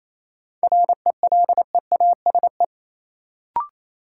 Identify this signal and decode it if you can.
Morse code, heard as a keyed tone, about 28 wpm, 700 Hz. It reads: RELEASE
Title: Sentences from Top 1100 Words - New Word Drill Alt - 28wpm